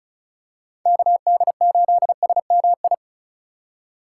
Morse code, heard as a keyed tone, about 36 wpm, 690 Hz. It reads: KD8SMI
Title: USA Call Signs - 6 Characters - 4x Character Spacing - 35wpm